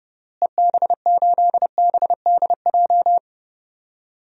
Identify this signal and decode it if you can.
Morse code, heard as a keyed tone, about 32 wpm, 700 Hz. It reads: EB8BDJ
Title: DXCC-100 Call Signs - 6 Characters - Repeated 3x - 30wpm